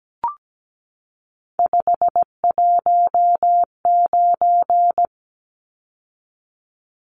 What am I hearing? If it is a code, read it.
Morse code, heard as a keyed tone, about 17 wpm, 700 Hz. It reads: 519
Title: Numbers - 3-Digit 17wpm